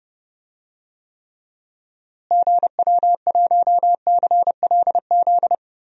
Morse code, heard as a keyed tone, about 30 wpm, 700 Hz. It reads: GW1CLZ
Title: DXCC-100 Call Signs - 6 Characters - Repeated 3x - 30wpm